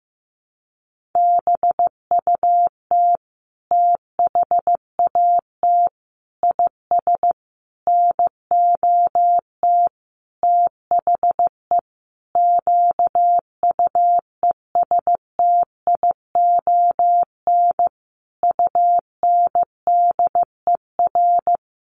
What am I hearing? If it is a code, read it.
Morse code, heard as a keyed tone, about 15 wpm, 700 Hz. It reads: BUT THAT IS NOT THE QUESTION UNDER
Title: Sentences from Top 1500 Words 15wpm